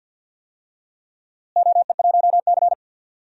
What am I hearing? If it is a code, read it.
Morse code, heard as a keyed tone, about 48 wpm, 700 Hz. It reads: OE1C
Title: DXCC-100 Call Signs - 4 Characters - Repeated 2x - 50wpm